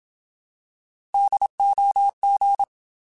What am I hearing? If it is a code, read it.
Morse code, heard as a keyed tone, about 27 wpm, 800 Hz. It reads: DOG
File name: Alphabetical / Alphabetical mp3s / Ds @25